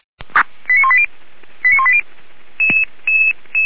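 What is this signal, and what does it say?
Five-tone selective call (ZVEI), heard as quick runs of five tones, tones of about 70 ms. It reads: 89180 89180
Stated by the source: ZVEI five-tone selective call received by SDR, on the server of Feuerwehr Orlamünde.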